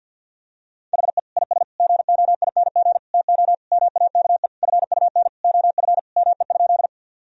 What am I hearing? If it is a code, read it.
Morse code, heard as a keyed tone, about 50 wpm, 700 Hz. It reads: HE IS GOING TO MAKE FUN OF ME?
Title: Sentences from Top 1000 Words - Review Part 4 - 50wpm